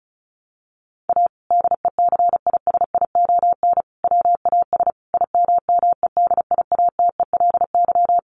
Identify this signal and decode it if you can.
Morse code, heard as a keyed tone, about 34 wpm, 700 Hz. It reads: A DECISION WAS IMMEDIATELY
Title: Sentences from Top 1100 Words 35wpm